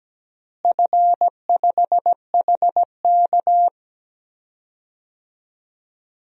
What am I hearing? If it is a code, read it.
Morse code, heard as a keyed tone, about 17 wpm, 700 Hz. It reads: F5HK
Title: DXCC-100 Call Signs - 4 Characters - Repeated 2x - 17wpm